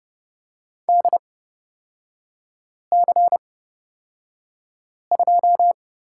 Morse code, heard as a keyed tone, about 29 wpm, spaced as for 5 wpm, 690 Hz. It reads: DC2
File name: course / 30:10wpm / 236 Lesson 38e - Letter q - Three Characters 30wpm